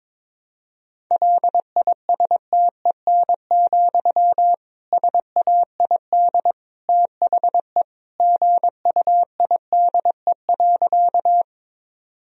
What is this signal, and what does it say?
Morse code, heard as a keyed tone, about 22 wpm, 700 Hz. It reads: LISTEN, SAID THE GUIDE.
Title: Sentences from Top 1000 Words - Review Part 3 - 22wpm